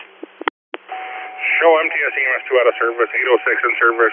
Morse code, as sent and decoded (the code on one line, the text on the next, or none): none